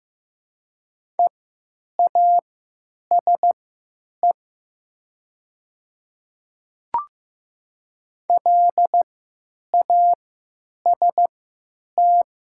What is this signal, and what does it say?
Morse code, heard as a keyed tone, about 15 wpm, spaced as for 9 wpm, 700 Hz. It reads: EASE LAST
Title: 3x Character Spacing - Top 300 Words - 15wpm